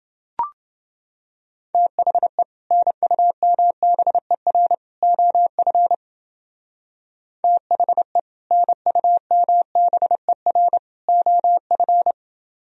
Morse code, heard as a keyed tone, about 28 wpm, 700 Hz. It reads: THE NUMBER OF THE NUMBER OF